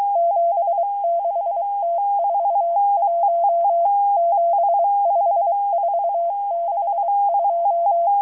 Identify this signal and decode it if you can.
Morse code, heard as a keyed tone, about 23 wpm, 680 Hz. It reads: 76T41 75462